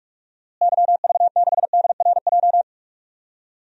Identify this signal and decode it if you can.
Morse code, heard as a keyed tone, about 46 wpm, 700 Hz. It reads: YV6DRJ
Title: DXCC-100 Call Signs - 6 Characters - Repeated 2x - 45wpm